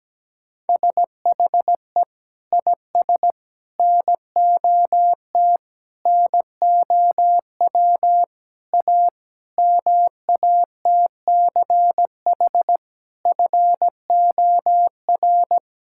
Morse code, heard as a keyed tone, about 17 wpm, 700 Hz. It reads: SHE IS NOT NOW A MATCH FOR
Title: Sentences from Top 1600 Words 17wpm